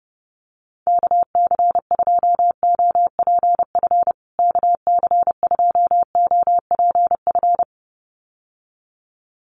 Morse code, heard as a keyed tone, about 30 wpm, 700 Hz. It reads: KC2OPF KC2OPF
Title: USA Call Signs Repeated 2x 30wpm